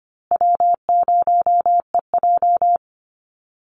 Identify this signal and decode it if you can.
Morse code, heard as a keyed tone, about 25 wpm, 700 Hz. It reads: W0EJ
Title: Easy Call Signs Repeated 3x - 25wpm